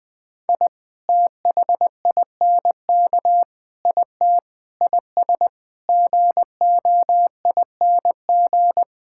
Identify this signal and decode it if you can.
Morse code, heard as a keyed tone, about 20 wpm, 700 Hz. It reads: I THINK IT IS GOING
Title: Sentences with 6-Word Phrases - Spoken Beginning - 20wpm